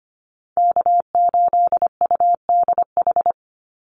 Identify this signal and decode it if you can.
Morse code, heard as a keyed tone, about 25 wpm, 700 Hz. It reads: K8UDH
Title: Call Signs Repeated 3x - 25wpm